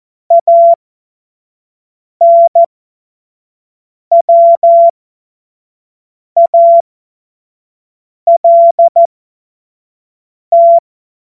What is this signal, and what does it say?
Morse code, heard as a keyed tone, about 14 wpm, spaced as for 5 wpm, 680 Hz. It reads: ANWALT
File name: Morsekurs / Witze / Witz 21. Juni 2022 25 F70